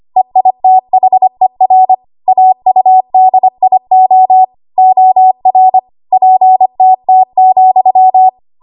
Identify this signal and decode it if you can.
Morse code, heard as a keyed tone, about 25 wpm, 750 Hz. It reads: EITHER AUDIO OR PTT,
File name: Archive / 25 WPM / 25WPM